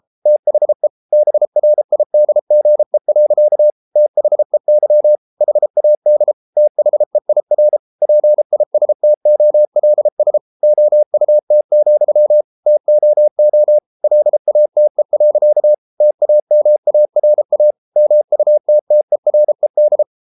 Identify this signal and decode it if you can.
Morse code, heard as a keyed tone, about 33 wpm, 600 Hz. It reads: THE BRIDGE. THEY HAD THEIR PISTOLS OUT, TOO LATE. TAMARA MUTTERED